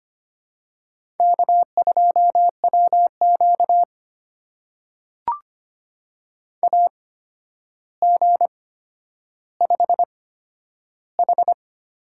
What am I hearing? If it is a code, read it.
Morse code, heard as a keyed tone, about 25 wpm, spaced as for 11 wpm, 700 Hz. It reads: K2WQ AG5H
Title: USA Call Signs - 4 Characters - 8x Character Spacing - 25wpm